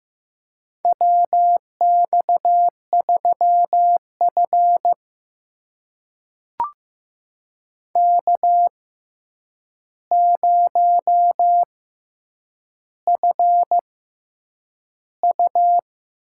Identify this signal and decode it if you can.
Morse code, heard as a keyed tone, about 15 wpm, spaced as for 8 wpm, 700 Hz. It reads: WX3F K0FU
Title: USA Call Signs - 4 Characters - 6x Character Spacing - 15wpm